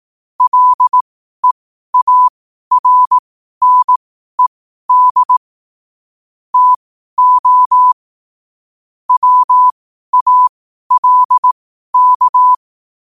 Morse code, heard as a keyed tone, about 18 wpm, spaced as for 12 wpm, 1000 Hz. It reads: LEARNED TO WALK
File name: qotdfiles / 12wpm